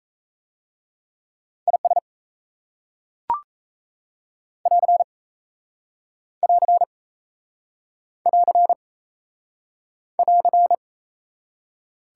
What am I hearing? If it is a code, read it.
Morse code, heard as a keyed tone, about 34 wpm, 700 Hz. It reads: IS <AR> <AR> <AR> <AR>
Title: Speed-Racing Top 180 QSO Elements 28wpm